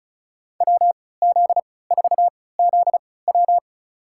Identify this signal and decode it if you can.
Morse code, heard as a keyed tone, about 36 wpm, spaced as for 20 wpm, 700 Hz. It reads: WZ4ZW
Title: USA Call Signs - 5 Characters - 3x Character Spacing - 35wpm